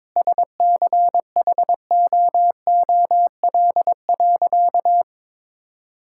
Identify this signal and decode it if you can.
Morse code, heard as a keyed tone, about 22 wpm, 700 Hz. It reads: SCHOOL.
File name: practice / 22wpm / Sentences with 4-Word Phrases - 22wpm